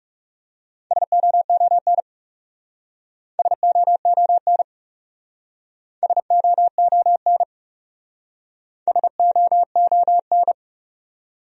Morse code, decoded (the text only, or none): SOON SOON SOON SOON